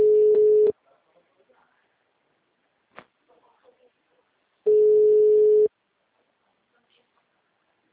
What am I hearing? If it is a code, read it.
Ringback tone (Europe, 425 Hz), with speech.